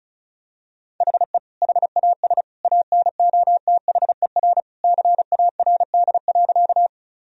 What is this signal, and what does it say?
Morse code, heard as a keyed tone, about 36 wpm, 700 Hz. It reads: HE HAS ANOTHER CARD.
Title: Sentences from Top 1000 Words - Review Part 4 - 35wpm